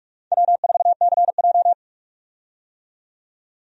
Morse code, heard as a keyed tone, about 42 wpm, 700 Hz. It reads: W4CJ
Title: USA Call Signs - 4 Characters - 45wpm